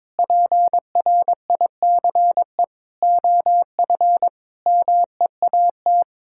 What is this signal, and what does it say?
Morse code, heard as a keyed tone, about 22 wpm, 700 Hz. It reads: PRICE OF MEAT
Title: Sentences from Top 1000 Words - Review Part 6 - 22wpm